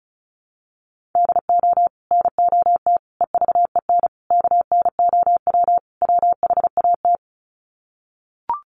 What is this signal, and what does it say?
Morse code, heard as a keyed tone, about 34 wpm, 700 Hz. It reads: DO NOT EVEN KNOW WHAT
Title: Sets of 5 Words Encore 35wpm